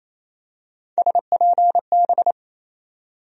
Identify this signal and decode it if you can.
Morse code, heard as a keyed tone, about 28 wpm, 700 Hz. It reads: SPB
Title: USA Call Sign Postfixes - 3 Letters 28wpm